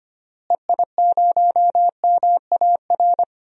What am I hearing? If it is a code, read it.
Morse code, heard as a keyed tone, about 25 wpm, 700 Hz. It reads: EI0MAR
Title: DXCC-100 Call Signs - 6 Characters - Repeated 3x - 25wpm